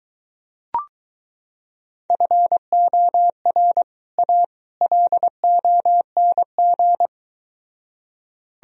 Morse code, heard as a keyed tone, about 22 wpm, 700 Hz. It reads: FOR A LONG